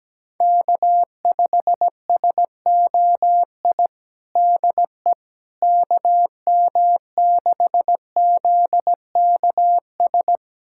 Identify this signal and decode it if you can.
Morse code, heard as a keyed tone, about 17 wpm, 700 Hz. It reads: K5SOI DE KM6ZKS
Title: USA Callsign DE Callsign 17wpm